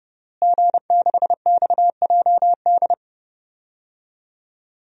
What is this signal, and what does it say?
Morse code, heard as a keyed tone, about 32 wpm, 700 Hz. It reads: G6XJD
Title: DXCC-100 Call Signs - 5 Characters - Repeated 2x - 30wpm